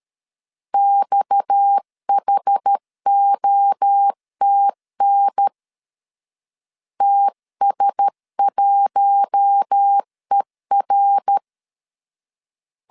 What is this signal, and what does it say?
Morse code, heard as a keyed tone, about 13 wpm, 800 Hz. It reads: XHOTN TS1ER
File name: CW / DET / 1 6 1500C S C 12wpm 3533